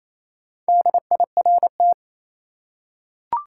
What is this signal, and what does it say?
Morse code, heard as a keyed tone, about 28 wpm, 700 Hz. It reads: DIRT
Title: Four Letter Words - 28wpm